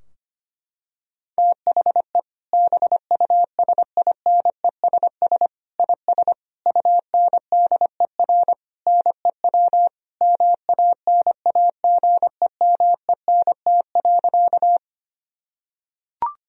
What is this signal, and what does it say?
Morse code, heard as a keyed tone, about 25 wpm, 700 Hz. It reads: THE BUSINESS IS UNDER NEW MANAGEMENT.
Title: Sentences from Top 1000 Words - Review Part 4 - 25wpm